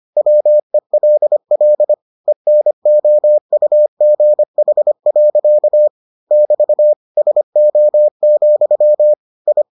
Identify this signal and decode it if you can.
Morse code, heard as a keyed tone, about 25 wpm, 600 Hz. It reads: WELL ENOUGH. <BT> SO, I